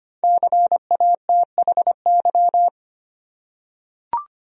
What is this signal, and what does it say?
Morse code, heard as a keyed tone, about 25 wpm, 700 Hz. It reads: CATHY
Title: USA Common Names - Max 5 Letters - 25wpm